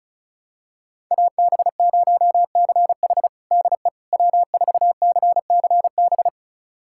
Morse code, heard as a keyed tone, about 36 wpm, 700 Hz. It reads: AB0CH DE W4CCB
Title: USA Callsign DE Callsign - 5 Characters - 35wpm